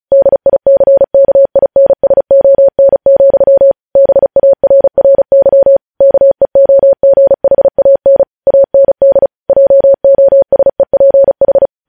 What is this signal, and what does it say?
Morse code, heard as a keyed tone, about 36 wpm, 550 Hz. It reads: DICKINSON, BARRY KEOGHAN AND JOSEPH
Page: News Headlines in Morse Code at 35 WPM